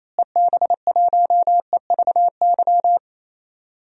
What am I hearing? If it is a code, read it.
Morse code, heard as a keyed tone, about 28 wpm, 700 Hz. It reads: EB1EVY
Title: DXCC-100 Call Signs - 6 Characters - Repeated 3x - 28wpm